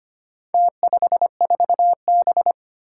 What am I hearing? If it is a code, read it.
Morse code, heard as a keyed tone, about 25 wpm, 700 Hz. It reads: T54B
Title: Four Challenging Letter-Number Combinations 25wpm